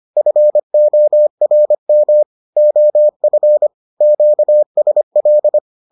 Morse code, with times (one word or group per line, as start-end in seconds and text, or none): 0.2-2.2 FORM
2.6-3.7 OF
4.0-5.6 QSL